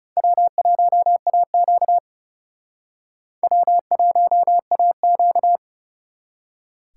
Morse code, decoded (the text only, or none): W1AQ W1AQ